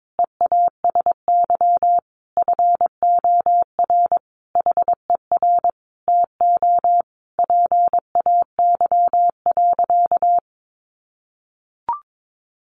Morse code, heard as a keyed tone, about 22 wpm, 700 Hz. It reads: EASY FOR HER TO PAY.
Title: Sentences from Top 1500 Words 22wpm